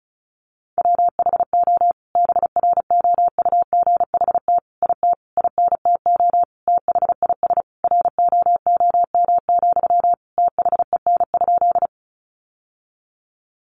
Morse code, heard as a keyed tone, about 34 wpm, 700 Hz. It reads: WHO BROUGHT IT INTO THIS ROOM, THEN?
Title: Sentences from Top 600 Words 35wpm